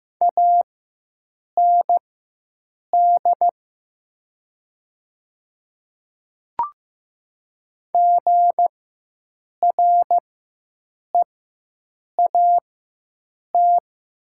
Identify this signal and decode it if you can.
Morse code, heard as a keyed tone, about 15 wpm, spaced as for 7 wpm, 700 Hz. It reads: AND GREAT
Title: 4x Character Spacing - Top 200 Words - 15wpm